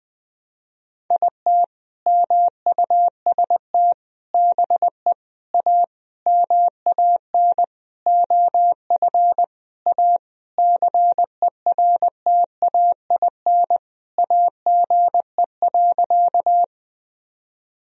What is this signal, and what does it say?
Morse code, heard as a keyed tone, about 20 wpm, 700 Hz. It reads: IT MUST BE A MAN OF A CERTAIN AGE.
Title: Sentences from Top 1000 Words - Review Part 3 - 20wpm